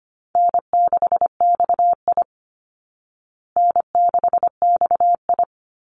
Morse code, heard as a keyed tone, about 25 wpm, 700 Hz. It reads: N6XI N6XI
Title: Call Signs Repeated 2x - 25wpm